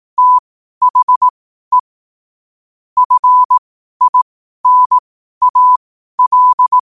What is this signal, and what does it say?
Morse code, heard as a keyed tone, about 18 wpm, spaced as for 12 wpm, 1000 Hz. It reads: THE FINAL
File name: qotdfiles / 12wpm